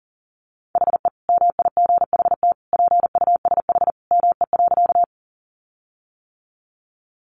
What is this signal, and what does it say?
Morse code, heard as a keyed tone, about 40 wpm, 700 Hz. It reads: HE MIGHT PUSH ME.
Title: Sentences from Top 1300 Words 40wpm